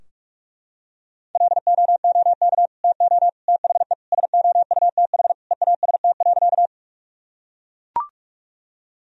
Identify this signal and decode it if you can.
Morse code, heard as a keyed tone, about 41 wpm, 700 Hz. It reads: LOOK TO THE SOUTH EAST.
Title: Sentences from Top 1000 Words - Review Part 3 - 45wpm